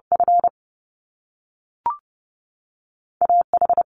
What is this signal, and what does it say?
Morse code, heard as a keyed tone, about 30 wpm, spaced as for 3 wpm, 700 Hz. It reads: FA H
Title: Top Two Letter Combinations in Words 30wpm